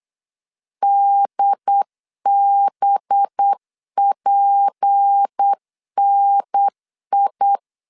Morse code, heard as a keyed tone, about 8 wpm, 800 Hz. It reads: DBPNI
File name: LEZIONI / 6 PKX0 / 6 P 8wpm 3025